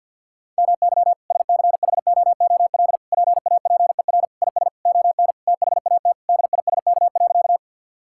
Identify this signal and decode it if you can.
Morse code, heard as a keyed tone, about 50 wpm, 700 Hz. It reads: MY SCHOOL PAPER IS ON THAT DISK.
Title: Sentences from Top 1000 Words - Review Part 6 - 50wpm